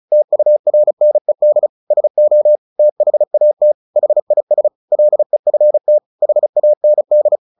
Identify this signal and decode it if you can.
Morse code, heard as a keyed tone, about 35 wpm, 600 Hz. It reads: TURNED SO THAT HIS LEFT HAND
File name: the old man and the sea 2 35wpm